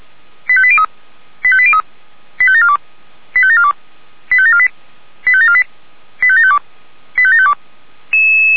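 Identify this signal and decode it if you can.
Five-tone selective call (ZVEI), heard as quick runs of five tones, tones of about 70 ms. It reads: 85803 85803 85742 85742 85748 85748 85752 85752
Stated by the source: Five-tone ZVEI selective call received by SDR.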